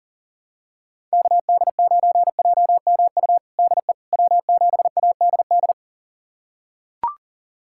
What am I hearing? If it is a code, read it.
Morse code, heard as a keyed tone, about 40 wpm, 700 Hz. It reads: KD9JMU DE W7ADD
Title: USA Callsign DE Callsign 40wpm